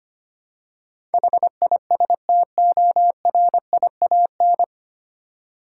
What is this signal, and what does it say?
Morse code, heard as a keyed tone, about 25 wpm, 700 Hz. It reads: HISTORIAN
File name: Nine Letter Words - 25wpm